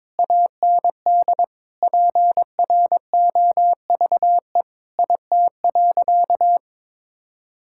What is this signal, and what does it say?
Morse code, heard as a keyed tone, about 22 wpm, 700 Hz. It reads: AND PROVE IT.
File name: practice / 22wpm / Sentences with 8-Word Phrases - 22wpm